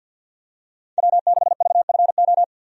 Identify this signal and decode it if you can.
Morse code, heard as a keyed tone, about 47 wpm, 700 Hz. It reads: W6VFO
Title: USA Call Signs - 5 Characters - 50wpm